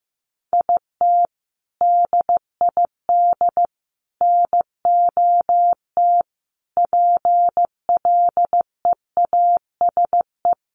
Morse code, heard as a keyed tone, about 15 wpm, 700 Hz. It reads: IT DID NOT PLEASE
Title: Sentences from Top 1500 Words 15wpm